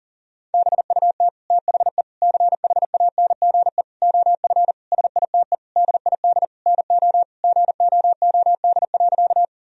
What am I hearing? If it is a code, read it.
Morse code, heard as a keyed tone, about 40 wpm, 700 Hz. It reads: BUT THE CHANGE OF SITE DID NO GOOD.
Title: Sentences from Top 1000 Words - Review Part 4 - 40wpm